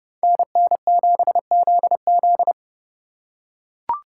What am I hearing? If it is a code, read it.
Morse code, heard as a keyed tone, about 29 wpm, 700 Hz. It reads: NN7ZZ